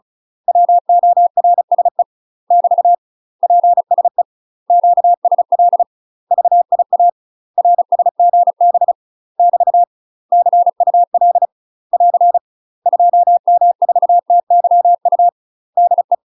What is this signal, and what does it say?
Morse code, heard as a keyed tone, about 34 wpm, 700 Hz. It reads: WORSE <BT> PSE QSL VIA RSGB <BT> CUL <AR> 2M4TYU DE
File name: UK MASTER QSOs TRAINING / 35WPM / UK MASTER QSOs 0025